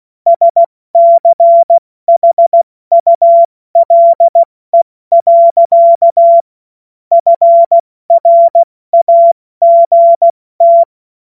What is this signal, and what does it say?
Morse code, heard as a keyed tone, about 16 wpm, 680 Hz. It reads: SCHULE. FRAGT